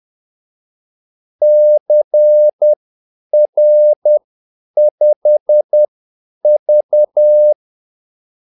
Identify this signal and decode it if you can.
Morse code, heard as a keyed tone, about 10 wpm, spaced as for 8 wpm, 600 Hz. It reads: CR5V
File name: Callsigns-10-8wpm